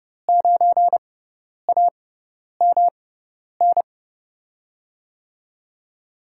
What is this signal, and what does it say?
Morse code, heard as a keyed tone, about 30 wpm, spaced as for 10 wpm, 700 Hz. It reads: 9AMN
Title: USA Call Signs - 5 Characters - 6x Character Spacing - 30wpm